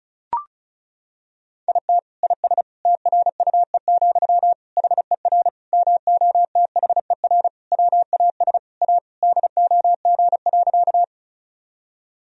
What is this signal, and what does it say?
Morse code, heard as a keyed tone, about 36 wpm, 700 Hz. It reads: IT IS TRUE, HER MOTHER WAS A DOG.
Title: Sentences from Top 1000 Words - Review Part 2 - 35wpm